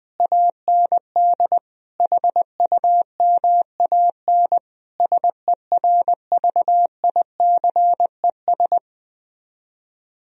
Morse code, heard as a keyed tone, about 20 wpm, 700 Hz. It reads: AND HUMAN SERVICES
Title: Speed-Racing - Sets of 5 Words 17wpm